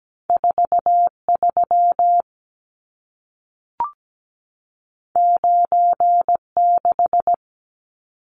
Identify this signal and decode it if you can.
Morse code, heard as a keyed tone, about 17 wpm, 700 Hz. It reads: 43 96
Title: Numbers - 2-Digit 17wpm